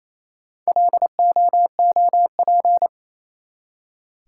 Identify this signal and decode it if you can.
Morse code, heard as a keyed tone, about 28 wpm, 700 Hz. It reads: LOOP
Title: Top 180 QSO Elements 28wpm